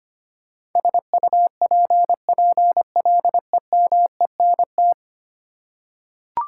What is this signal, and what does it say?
Morse code, heard as a keyed tone, about 25 wpm, 700 Hz. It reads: SUPPLEMENT
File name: Ten Letter Words - 25wpm